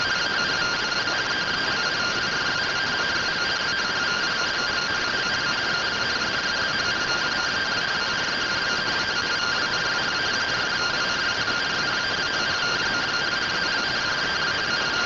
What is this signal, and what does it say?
Two-tone teleprinter signal, about 170 Hz shift.
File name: findings / MR CORONAL fr1 -131.gen-34 rtty rx